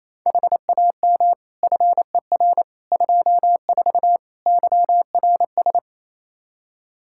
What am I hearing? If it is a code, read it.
Morse code, heard as a keyed tone, about 28 wpm, 700 Hz. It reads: HAM FER 24 YRS
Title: Practice QSO - Exchange 3 - More Info - 28wpm